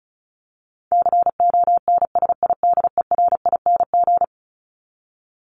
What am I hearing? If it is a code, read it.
Morse code, heard as a keyed tone, about 34 wpm, 700 Hz. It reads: CONSIDERING
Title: ING Suffix - Top 100 Words with Base Words 35wpm